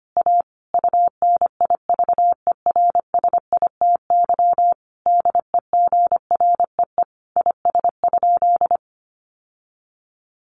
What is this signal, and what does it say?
Morse code, heard as a keyed tone, about 25 wpm, 700 Hz. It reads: A UNIVERSITY DEGREE IS?
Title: Sentences from Top 1500 Words 25wpm